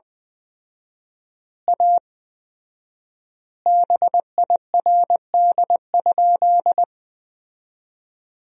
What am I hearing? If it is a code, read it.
Morse code, heard as a keyed tone, about 20 wpm, 700 Hz. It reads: A BIRD?
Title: 4x Word-Spacing - Sentences from Top 400 Words - 20wpm